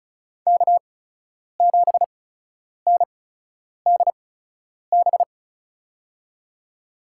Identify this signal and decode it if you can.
Morse code, heard as a keyed tone, about 34 wpm, spaced as for 10 wpm, 700 Hz. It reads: K7NDB
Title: USA Call Signs - 5 Characters - 8x Character Spacing - 35wpm